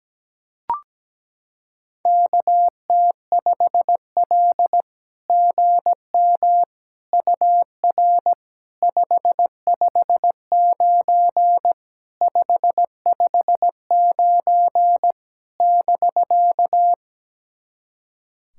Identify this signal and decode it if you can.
Morse code, heard as a keyed tone, about 17 wpm, 700 Hz. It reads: KT5L GM UR 559 559 <BK>
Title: SOTA QSO Practice 17wpm